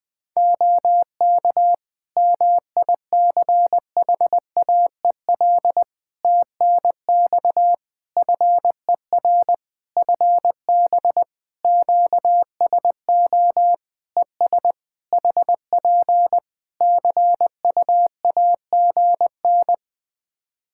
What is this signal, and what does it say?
Morse code, heard as a keyed tone, about 21 wpm, 700 Hz. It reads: OK MICHAEL TNX FER FB QSO ES HP CUAGN
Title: K1USN SST QSO Practice 20wpm